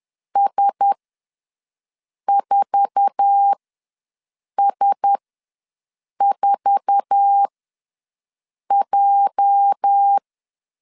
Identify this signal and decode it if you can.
Morse code, heard as a keyed tone, about 10 wpm, spaced as for 5 wpm, 800 Hz. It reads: S4S4J
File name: LEZIONI / 8 YJ45 / 8 YJ4 ws 10wpm 4020